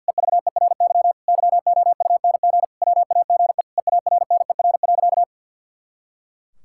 Morse code, heard as a keyed tone, about 50 wpm, 700 Hz. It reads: EVERY YOUNG WAGE EARNER.
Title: Pangram Challenge v2 - 50wpm